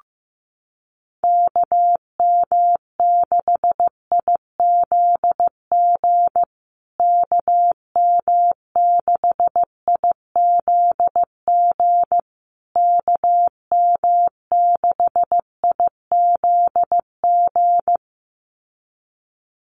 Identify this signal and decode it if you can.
Morse code, heard as a keyed tone, about 15 wpm, 700 Hz. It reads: KM6IZG KM6IZG KM6IZG
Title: USA Call Signs Repeated 3x 15wpm